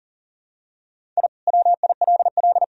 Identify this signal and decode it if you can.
Morse code, heard as a keyed tone, about 40 wpm, 700 Hz. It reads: I WILL
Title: Sentences from Top 1000 Words - Review Part 1 - 40wpm